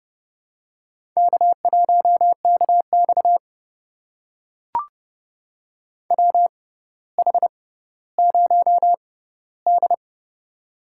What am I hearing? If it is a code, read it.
Morse code, heard as a keyed tone, about 31 wpm, spaced as for 15 wpm, 700 Hz. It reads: K1KX WH0D
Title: USA Call Signs - 4 Characters - 6x Character Spacing - 30wpm